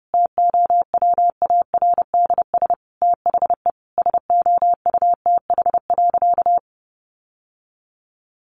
Morse code, heard as a keyed tone, about 30 wpm, 700 Hz. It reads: TOWARDS THE SOUTH.
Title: Sentences from Top 1100 Words 30wpm